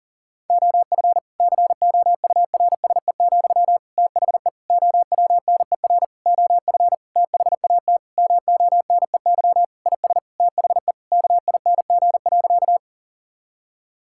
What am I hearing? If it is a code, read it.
Morse code, heard as a keyed tone, about 40 wpm, 700 Hz. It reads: OF COURSE, THE OWNER OF THAT MONEY IS THE KING.
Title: Sentences from Top 1600 Words 40wpm